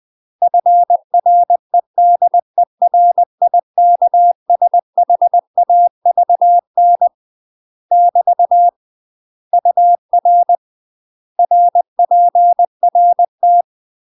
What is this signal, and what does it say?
Morse code, heard as a keyed tone, about 20 wpm, 700 Hz. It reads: FREDERIKSHAVN <BT> UR RPRT
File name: UK MASTER QSOs TRAINING / 20WPM / UK MASTER QSOs 0027